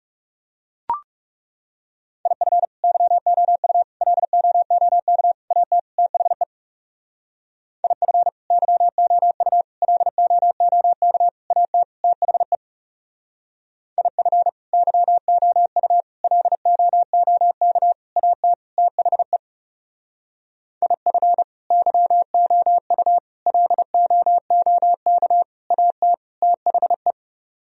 Morse code, heard as a keyed tone, about 37 wpm, 700 Hz. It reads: IF YOU LOOK AT THE IF YOU LOOK AT THE IF YOU LOOK AT THE IF YOU LOOK AT THE
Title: Speed-Racing - Sets of 5 Words 30wpm